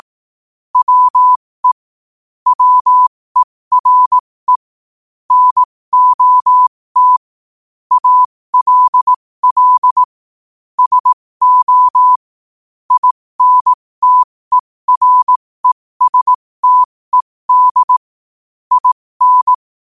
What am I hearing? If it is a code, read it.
Morse code, heard as a keyed tone, about 18 wpm, spaced as for 15 wpm, 1000 Hz. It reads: WE WERE NOT ALL SO INTERESTED IN
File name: qotdfiles / 15wpm